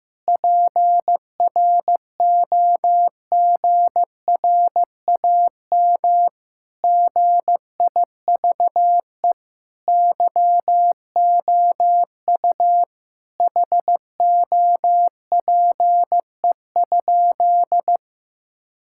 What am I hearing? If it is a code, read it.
Morse code, heard as a keyed tone, about 15 wpm, 700 Hz. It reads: PROGRAM GIVE YOU HOPE?
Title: Sentences from Top 1000 Words - Review Part 5 - 15wpm